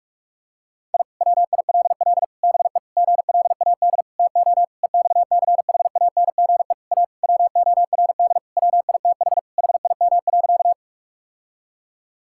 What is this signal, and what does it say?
Morse code, heard as a keyed tone, about 41 wpm, 700 Hz. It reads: I WILL BE GLAD TO EXCHANGE A WORD WITH HIM.
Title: Sentences from Top 1000 Words - Review Part 6 - 45wpm